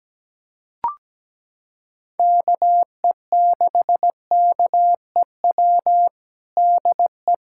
Morse code, heard as a keyed tone, about 17 wpm, 700 Hz. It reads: KE6KEW DE
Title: USA Callsign DE Callsign 17wpm